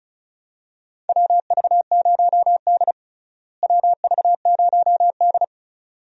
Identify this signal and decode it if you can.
Morse code, heard as a keyed tone, about 36 wpm, 700 Hz. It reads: WV0D WV0D
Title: USA Call Signs - 4 Characters - Repeated 2x - 35wpm